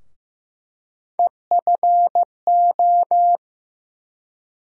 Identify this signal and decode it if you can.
Morse code, heard as a keyed tone, about 15 wpm, 700 Hz. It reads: EFO
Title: USA Call Sign Postfixes - 3 Letters 15wpm